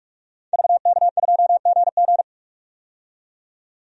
Morse code, heard as a keyed tone, about 46 wpm, 700 Hz. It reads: VK2GG
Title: DXCC-100 Call Signs - 5 Characters - 45wpm